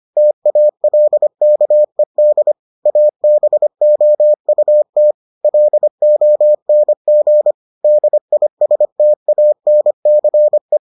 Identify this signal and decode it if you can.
Morse code, heard as a keyed tone, about 25 wpm, 600 Hz. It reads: TALKED ABOUT LONG DISTANCE